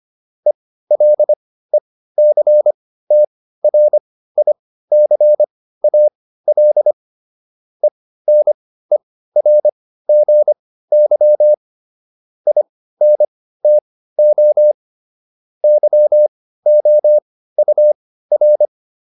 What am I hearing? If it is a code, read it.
Morse code, heard as a keyed tone, about 25 wpm, spaced as for 15 wpm, 600 Hz. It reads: ELECTRICAL ENERGY INTO YOUR